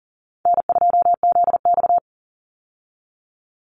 Morse code, heard as a keyed tone, about 40 wpm, 700 Hz. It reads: N2ZX
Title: Easy Call Signs Repeated 3x - 40wpm